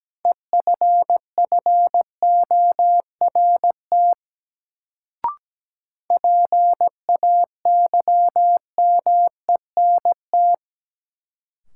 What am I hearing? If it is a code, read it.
Morse code, heard as a keyed tone, about 17 wpm, 700 Hz. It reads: EFFORT PAYMENT
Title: Sentences from Top 1200 Words - New Word Drill - 17wpm